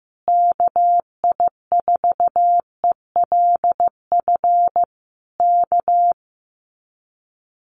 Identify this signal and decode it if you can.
Morse code, heard as a keyed tone, about 15 wpm, 700 Hz. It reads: KI4ELF K
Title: Common QSO Phrases Encore 15wpm